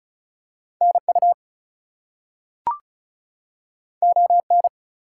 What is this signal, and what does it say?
Morse code, heard as a keyed tone, about 36 wpm, 700 Hz. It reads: NU ON
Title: US States and Canadian Provinces 35wpm